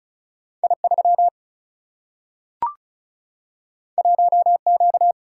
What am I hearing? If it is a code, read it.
Morse code, heard as a keyed tone, about 36 wpm, 700 Hz. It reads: I3 1Q
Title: Single-Letter Single-Number 35wpm